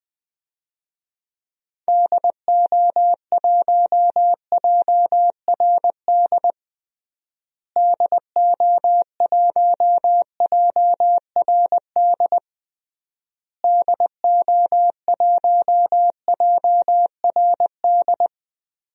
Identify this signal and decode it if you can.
Morse code, heard as a keyed tone, about 21 wpm, 700 Hz. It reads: DO1JRD DO1JRD DO1JRD
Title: DXCC-100 Call Signs - 6 Characters - Repeated 3x - 20wpm